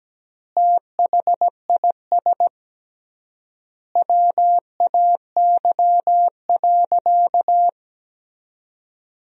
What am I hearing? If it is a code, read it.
Morse code, heard as a keyed tone, about 17 wpm, 700 Hz. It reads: THIS WAY.
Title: 3x Word-Spacing - Sentences from Top 100 Words - 17wpm